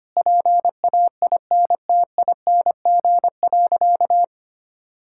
Morse code, heard as a keyed tone, about 25 wpm, 700 Hz. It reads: PAINTING.
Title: Sentences from Top 1000 Words - Review Part 6 - 25wpm